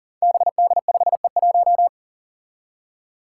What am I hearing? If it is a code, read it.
Morse code, heard as a keyed tone, about 40 wpm, 700 Hz. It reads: BD5E1
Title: Five Challenging Letter-Number Combinations 40wpm